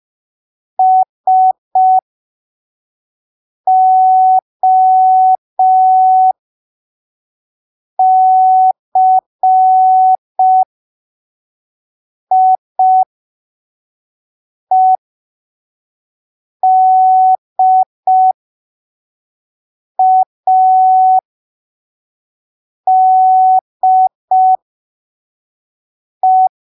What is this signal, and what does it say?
Morse code, heard as a keyed tone, about 5 wpm, spaced as for 3 wpm, 750 Hz. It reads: SOCIEDADE
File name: arquivos / texto / texto 30 05ppm